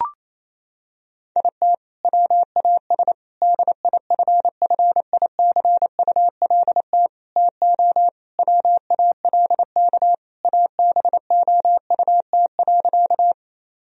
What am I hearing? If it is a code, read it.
Morse code, heard as a keyed tone, about 28 wpm, 700 Hz. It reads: IT WAS DIFFICULT TO WALK ABOUT.
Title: Sentences from Top 1000 Words - Review Part 1 - 28wpm